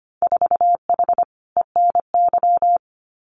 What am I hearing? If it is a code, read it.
Morse code, heard as a keyed tone, about 25 wpm, 690 Hz. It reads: 4H ENY
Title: Winter Field Day Exchanges Encore 25wpm